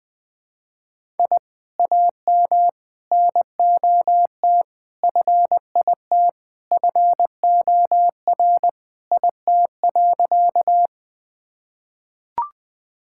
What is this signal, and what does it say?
Morse code, heard as a keyed tone, about 20 wpm, 700 Hz. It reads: I AM NOT FIT FOR IT.
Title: Sentences from Top 1000 Words - Review Part 5 - 20wpm